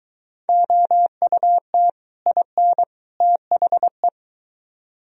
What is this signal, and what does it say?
Morse code, heard as a keyed tone, about 23 wpm, 700 Hz. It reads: OUT IN THE